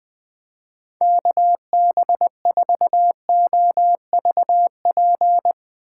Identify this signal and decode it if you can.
Morse code, heard as a keyed tone, about 20 wpm, 700 Hz. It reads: KB4OVP
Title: USA Call Signs - 2x Character Spacing - 20wpm